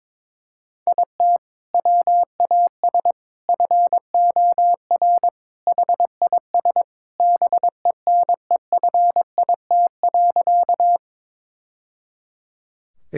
Morse code, heard as a keyed tone, about 22 wpm, 700 Hz. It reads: IT WAS FOR HIS BENEFIT.
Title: Sentences from Top 1000 Words - Review Part 6 - 22wpm